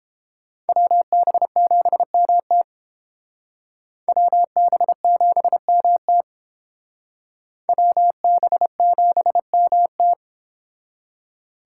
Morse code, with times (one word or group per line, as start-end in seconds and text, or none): 0.7-2.6 WB7MT
4.1-6.2 WB7MT
7.7-10.1 WB7MT